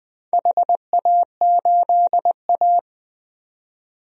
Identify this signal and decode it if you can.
Morse code, heard as a keyed tone, about 20 wpm, 700 Hz. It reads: HA8A
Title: DXCC-100 Call Signs - 4 Characters - 20wpm